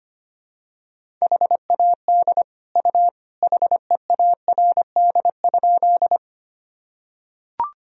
Morse code, heard as a keyed tone, about 25 wpm, 700 Hz. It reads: HAD U HEARD?
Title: Sentences from Top 400 Words and Abbreviations 25wpm